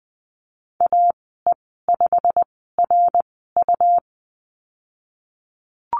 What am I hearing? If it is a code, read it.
Morse code, heard as a keyed tone, about 20 wpm, spaced as for 15 wpm, 700 Hz. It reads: AE5RU
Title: 147 Lesson 24g - Letter y - Call Signs 20wpm